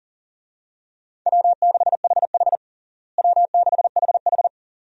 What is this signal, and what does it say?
Morse code, heard as a keyed tone, about 40 wpm, 700 Hz. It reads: W6HH W6HH
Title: USA Call Signs - 4 Characters - Repeated 2x - 40wpm